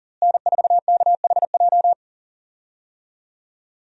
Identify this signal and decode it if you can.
Morse code, heard as a keyed tone, about 40 wpm, 700 Hz. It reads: N4KHJ
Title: USA Call Signs - 5 Characters - Repeated 2x - 40wpm